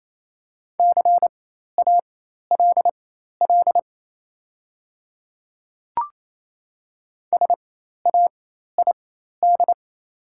Morse code, heard as a keyed tone, about 28 wpm, spaced as for 13 wpm, 700 Hz. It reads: CALL SAID
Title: 4x Character Spacing - Top 100 Words - 28wpm